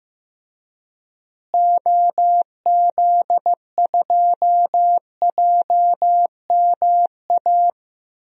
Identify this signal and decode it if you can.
Morse code, heard as a keyed tone, about 15 wpm, 700 Hz. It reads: OZ2JMA
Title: DXCC-100 Call Signs - 6 Characters - Repeated 3x - 15wpm